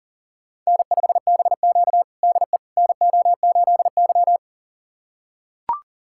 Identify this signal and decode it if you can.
Morse code, heard as a keyed tone, about 40 wpm, 700 Hz. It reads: N5BQ DE NO8Y
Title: USA Callsign DE Callsign - 4 Characters - 40wpm